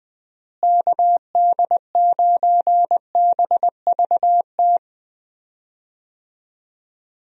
Morse code, heard as a keyed tone, about 20 wpm, 700 Hz. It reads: KD9BVT
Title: USA Call Signs - 6 Characters - Repeated 2x - 20wpm